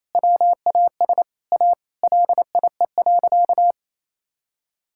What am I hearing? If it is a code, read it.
Morse code, heard as a keyed tone, about 28 wpm, 700 Hz. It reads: WAS A LIE.
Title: Sentences from Top 1600 Words 28wpm